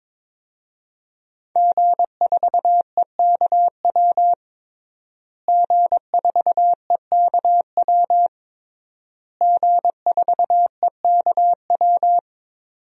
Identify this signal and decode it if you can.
Morse code, heard as a keyed tone, about 22 wpm, 700 Hz. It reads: G4EKW G4EKW G4EKW
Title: DXCC-100 Call Signs - 5 Characters - Repeated 3x - 22wpm